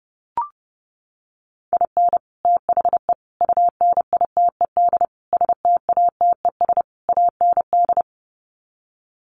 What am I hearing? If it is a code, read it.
Morse code, heard as a keyed tone, about 30 wpm, 700 Hz. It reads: IN THE UNITED STATES AND
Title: Sets of 5 Words 30wpm